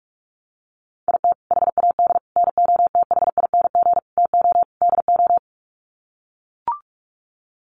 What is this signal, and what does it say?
Morse code, heard as a keyed tone, about 44 wpm, 690 Hz. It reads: IT HAD NOTHING TO DO
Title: Sets of 5 Words Encore2 45wpm